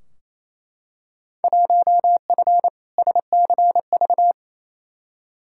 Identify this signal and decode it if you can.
Morse code, heard as a keyed tone, about 28 wpm, 700 Hz. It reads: 1F SCV
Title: Field Day Exchanges Encore - 28wpm